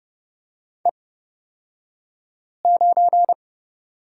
Morse code, heard as a keyed tone, about 30 wpm, spaced as for 5 wpm, 700 Hz. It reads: E9
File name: Lesson 37d - Letter x - Two Characters 30wpm